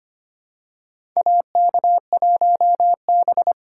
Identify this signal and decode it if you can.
Morse code, heard as a keyed tone, about 25 wpm, 700 Hz. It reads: AK1B
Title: USA Call Signs - 4 Characters - 2x Character Spacing - 25wpm